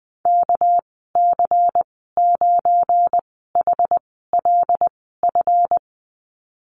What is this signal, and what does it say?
Morse code, heard as a keyed tone, about 20 wpm, spaced as for 15 wpm, 700 Hz. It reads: KC9HLF
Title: 200 Lesson 32g - Letter k - Call Signs 20wpm